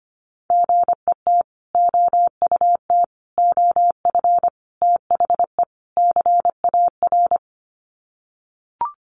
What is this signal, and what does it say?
Morse code, heard as a keyed tone, about 25 wpm, 700 Hz. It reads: GET OUT OF THE CAR
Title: Sets of 5 Words Encore2 25wpm